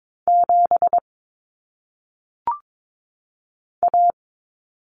Morse code, heard as a keyed tone, about 22 wpm, 700 Hz. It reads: 7 A
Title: Common Cut Numbers - 1-Digit 22wpm